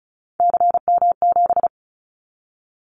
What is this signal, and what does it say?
Morse code, heard as a keyed tone, about 36 wpm, 700 Hz. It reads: CM7
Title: Callsign Prefixes for Top 100 DXCC - 35wpm